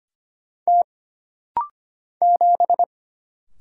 Morse code, heard as a keyed tone, about 25 wpm, 700 Hz. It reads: T 7
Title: Single Letter-Number - Rapid-Fire 25wpm